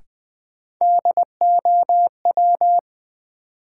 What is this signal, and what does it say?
Morse code, heard as a keyed tone, about 20 wpm, 700 Hz. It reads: DOW
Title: Sets of 5 Words - New Word Drill - 20wpm